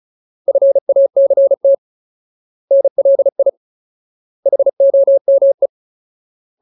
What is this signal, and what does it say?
Morse code, heard as a keyed tone, about 36 wpm, 550 Hz. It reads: FACT NLI HOME
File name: Ultimate Word List / 35WPM Ultimate